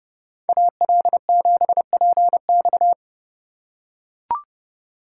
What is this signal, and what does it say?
Morse code, heard as a keyed tone, about 30 wpm, 700 Hz. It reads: AL7PX
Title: USA Call Signs - 5 Characters - 3x Character Spacing - 30wpm